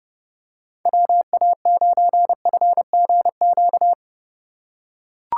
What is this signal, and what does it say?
Morse code, heard as a keyed tone, about 31 wpm, 700 Hz. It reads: WA9FGQ